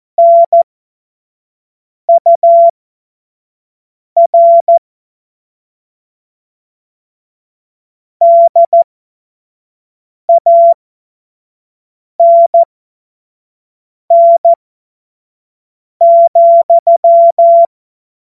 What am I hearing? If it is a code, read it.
Morse code, heard as a keyed tone, about 14 wpm, spaced as for 5 wpm, 680 Hz. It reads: NUR DANN,
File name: Morsekurs / Witze / Witz 13. Juli 2021, 25 F70